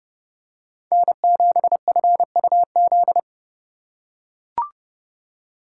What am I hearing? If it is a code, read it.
Morse code, heard as a keyed tone, about 31 wpm, 700 Hz. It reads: N7FUZ